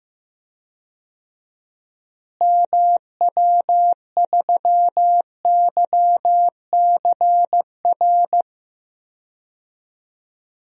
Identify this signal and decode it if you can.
Morse code, heard as a keyed tone, about 15 wpm, 700 Hz. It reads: MW3YCR
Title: DXCC-100 Call Signs - 6 Characters - Repeated 3x - 15wpm